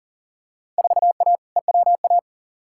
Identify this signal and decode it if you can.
Morse code, heard as a keyed tone, about 40 wpm, 700 Hz. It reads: 4A EWA
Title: Field Day Exchanges Encore - 40wpm